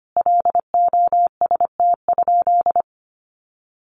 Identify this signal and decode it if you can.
Morse code, heard as a keyed tone, about 25 wpm, 700 Hz. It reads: LOST?
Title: Sentences from Top 1400 Words 25wpm